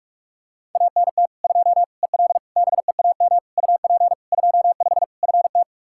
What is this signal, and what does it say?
Morse code, heard as a keyed tone, about 41 wpm, 700 Hz. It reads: ANT 2 EL BEAM UP 25 FT
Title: Practice QSO - Exchange 2 - Info - 45wpm